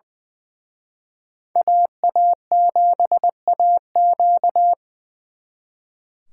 Morse code, heard as a keyed tone, about 24 wpm, 700 Hz. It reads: AA7AQ
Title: Speed Racing - USA 2x2 Call Signs 20wpm